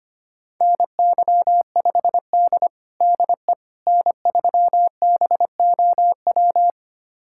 Morse code, heard as a keyed tone, about 25 wpm, 700 Hz. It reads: NY5D DE N3BOW
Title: USA Callsign DE Callsign 25wpm